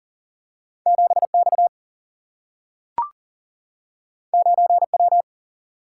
Morse code, heard as a keyed tone, about 40 wpm, 700 Hz. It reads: 7 X 9 W
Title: Single-Letter Single-Number 40wpm